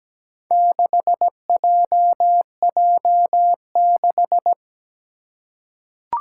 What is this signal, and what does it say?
Morse code, heard as a keyed tone, about 17 wpm, 700 Hz. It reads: 6JJ6
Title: Four Challenging Letter-Number Combinations 17wpm